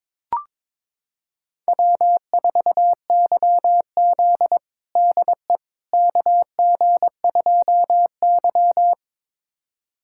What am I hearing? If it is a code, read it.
Morse code, heard as a keyed tone, about 21 wpm, 700 Hz. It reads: W4YZ DE KG2Y
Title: USA Callsign DE Callsign - 4 Characters - 22wpm